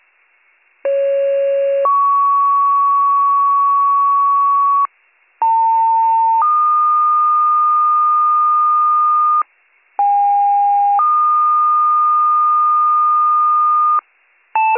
Two-tone page, heard as a one-second tone, then a three-second tone, about 550 then 1120 Hz, about 900 then 1220 Hz, about 810 then 1220 Hz. It is heard from 0.8 to 14.0 s.